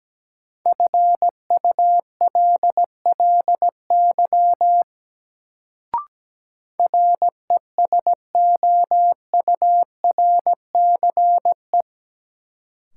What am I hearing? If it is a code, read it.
Morse code, heard as a keyed tone, about 17 wpm, 700 Hz. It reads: FULLY RESOURCE
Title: Sentences from Top 1300 Words - New Word Drill - 17wpm